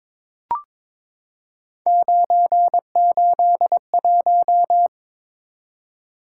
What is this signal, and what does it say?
Morse code, heard as a keyed tone, about 22 wpm, 700 Hz. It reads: 981
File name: practice / 22wpm / Three Letter-Number Combinations - 22wpm